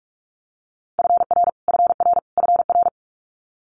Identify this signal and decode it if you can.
Morse code, heard as a keyed tone, about 46 wpm, 700 Hz. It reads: FR FR FR
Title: Multi-letter Phonemes Repeated 3x 45wpm